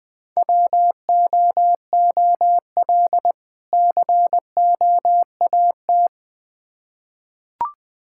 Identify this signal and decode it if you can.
Morse code, heard as a keyed tone, about 20 wpm, 700 Hz. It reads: WOOL COAT
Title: Difficult Words with Hints - 20wpm